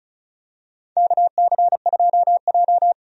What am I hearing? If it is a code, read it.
Morse code, heard as a keyed tone, about 36 wpm, 700 Hz. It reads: KC2J
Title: USA Call Signs - 4 Characters - 35wpm